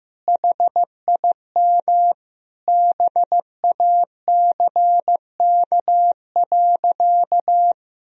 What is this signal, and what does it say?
Morse code, heard as a keyed tone, about 15 wpm, 700 Hz. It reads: HIM BACK.
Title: Sentences from Top 1000 Words - Review Part 3 - 15wpm